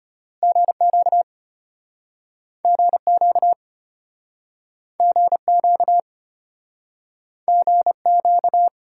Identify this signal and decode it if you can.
Morse code, heard as a keyed tone, about 34 wpm, 700 Hz. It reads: GQ GQ GQ GQ